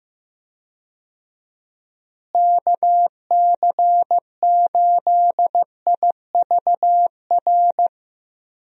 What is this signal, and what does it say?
Morse code, heard as a keyed tone, about 15 wpm, 700 Hz. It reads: KC8IVR
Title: USA Call Signs - 6 Characters - Repeated 2x - 15wpm